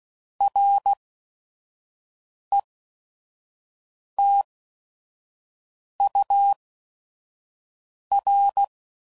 Morse code, heard as a keyed tone, about 16 wpm, spaced as for 5 wpm, 800 Hz. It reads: RETUR
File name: CW Code Groups @5WPM / Technical Subjects / XC